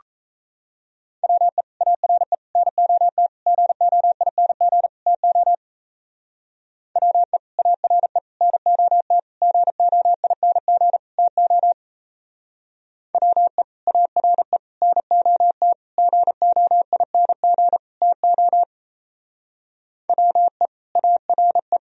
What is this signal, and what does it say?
Morse code, heard as a keyed tone, about 33 wpm, 700 Hz. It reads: WE ARE NOT GOING TO WE ARE NOT GOING TO WE ARE NOT GOING TO WE ARE
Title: Speed-Racing - Sets of 5 Words 28wpm